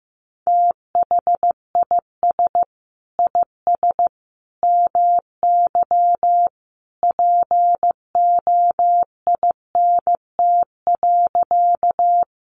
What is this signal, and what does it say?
Morse code, heard as a keyed tone, about 15 wpm, 700 Hz. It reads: THIS IS MY POINT.
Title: Sentences from Top 200 Words 15wpm